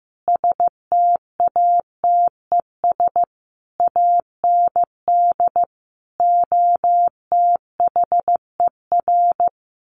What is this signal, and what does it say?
Morse code, heard as a keyed tone, about 15 wpm, 700 Hz. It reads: STATES AND OTHER
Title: Sets of 5 Words Encore 15wpm